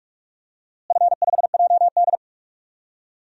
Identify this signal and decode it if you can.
Morse code, heard as a keyed tone, about 46 wpm, 700 Hz. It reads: F5JD
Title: DXCC-100 Call Signs - 4 Characters - Repeated 3x - 45wpm